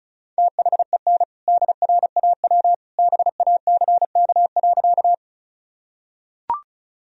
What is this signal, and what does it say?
Morse code, heard as a keyed tone, about 36 wpm, 700 Hz. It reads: THEN DRAW BACK.
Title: Sentences from Top 1000 Words - Review Part 1 - 35wpm